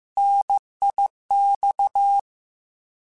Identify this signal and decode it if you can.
Morse code, heard as a keyed tone, about 15 wpm, 800 Hz. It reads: NIX
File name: Contrasts / Contrasts-mp3s / 3LtrContrasts @13